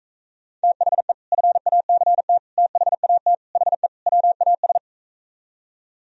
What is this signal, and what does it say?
Morse code, heard as a keyed tone, about 45 wpm, 700 Hz. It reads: THE FACT THAT HE WAS